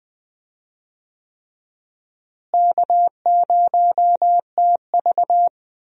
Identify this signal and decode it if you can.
Morse code, heard as a keyed tone, about 20 wpm, 700 Hz. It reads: K0TV
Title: USA Call Signs - 4 Characters - Repeated 3x - 20wpm